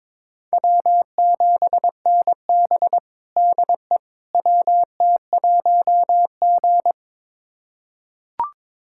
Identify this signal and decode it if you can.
Morse code, heard as a keyed tone, about 22 wpm, 700 Hz. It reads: W7NB DE WT1G
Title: USA Callsign DE Callsign - 4 Characters - 22wpm